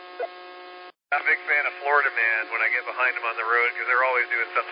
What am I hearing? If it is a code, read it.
Emergency Alert System (SAME) tones, heard as harsh buzzing bursts, with speech. It is heard from 1.2 to 4.0 s.